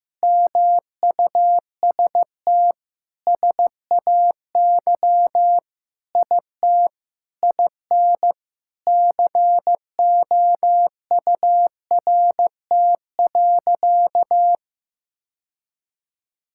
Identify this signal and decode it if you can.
Morse code, heard as a keyed tone, about 15 wpm, 700 Hz. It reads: MUST SAY IT IN COURT.
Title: Sentences from Top 1600 Words 15wpm